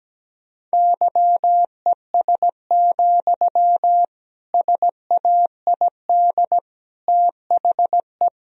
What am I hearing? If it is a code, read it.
Morse code, heard as a keyed tone, about 17 wpm, 700 Hz. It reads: YES, SAID THE
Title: Sentences from Top 1000 Words - Review Part 4 - 17wpm